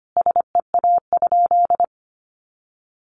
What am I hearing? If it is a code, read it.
Morse code, heard as a keyed tone, about 25 wpm, 700 Hz. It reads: SEA?
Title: Sentences from Top 500 Words 25wpm